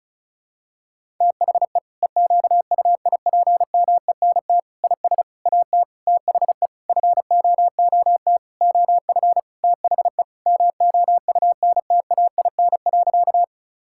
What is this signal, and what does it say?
Morse code, heard as a keyed tone, about 36 wpm, 700 Hz. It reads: THE EQUIPMENT IS AT THE FOOT OF THE MOUNTAIN.
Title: Sentences from Top 1000 Words - Review Part 5 - 35wpm